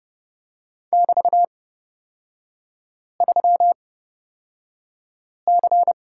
Morse code, heard as a keyed tone, about 30 wpm, spaced as for 5 wpm, 700 Hz. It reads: <BT>3C
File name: Lesson 36e - BT - Three Characters 30wpm